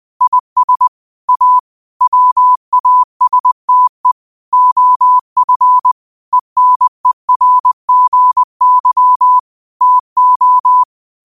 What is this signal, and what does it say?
Morse code, heard as a keyed tone, about 20 wpm, 1000 Hz. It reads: IS A WASTE OF ENERGY TO
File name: qotdfiles / 20wpm